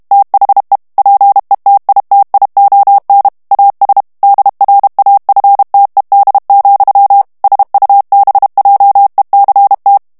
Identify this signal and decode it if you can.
Morse code, heard as a keyed tone, about 32 wpm, 800 Hz. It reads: THE PETITION AS DRAFTED, SUBJECT
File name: morse mp3 / ANGLAIS texte clair 3 vitesse 30 mots minute